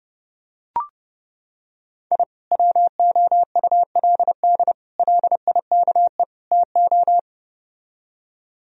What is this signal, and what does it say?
Morse code, heard as a keyed tone, about 30 wpm, 700 Hz. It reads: I WOULD LIKE TO